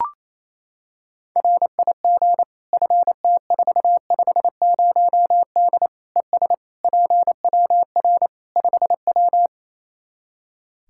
Morse code, heard as a keyed tone, about 27 wpm, 700 Hz. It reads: RIG FT450D ES PWR 5W
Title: Practice QSO - Exchange 2 - Info - 28wpm